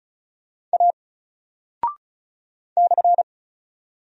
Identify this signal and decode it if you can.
Morse code, heard as a keyed tone, about 36 wpm, 700 Hz. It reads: A /
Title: Single Character (Letter:Number:Punctuation:Prosign) 35wpm